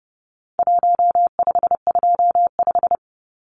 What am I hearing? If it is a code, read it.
Morse code, heard as a keyed tone, about 30 wpm, 700 Hz. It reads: 1525
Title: Numbers - Contest Style - 4-Digit 30wpm